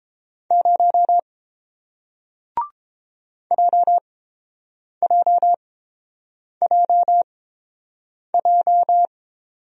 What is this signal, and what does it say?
Morse code, heard as a keyed tone, about 31 wpm, spaced as for 7 wpm, 700 Hz. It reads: 0 JJJJ